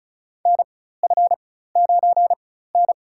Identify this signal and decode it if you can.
Morse code, heard as a keyed tone, about 34 wpm, spaced as for 16 wpm, 700 Hz. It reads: NF9N
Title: USA Call Signs - 4 Characters - 4x Character Spacing - 35wpm